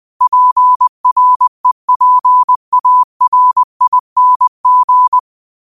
Morse code, heard as a keyed tone, about 20 wpm, 1000 Hz. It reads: PREPARING